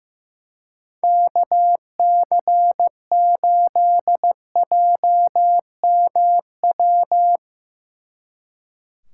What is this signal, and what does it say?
Morse code, heard as a keyed tone, about 16 wpm, 700 Hz. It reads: KC8JMW
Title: Speed Racing - USA Call Signs 15wpm